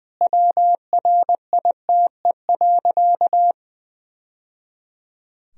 Morse code, heard as a keyed tone, about 20 wpm, 700 Hz. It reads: WRITE.
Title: Sentences from Top 1000 Words - Review Part 1 - 20wpm